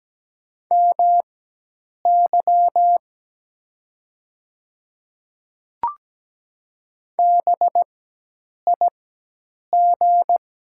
Morse code, heard as a keyed tone, about 17 wpm, spaced as for 8 wpm, 690 Hz. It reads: MY BIG
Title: 4x Character Spacing - Top 300 Words - 17wpm